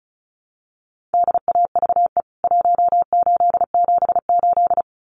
Morse code, heard as a keyed tone, about 34 wpm, 700 Hz. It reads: DAVE 1878
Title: CWT Exchange 35wpm